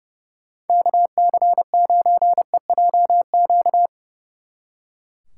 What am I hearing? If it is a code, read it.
Morse code, heard as a keyed tone, about 29 wpm, 700 Hz. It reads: KC9EJQ
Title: Speed Racing - USA Call Signs 30wpm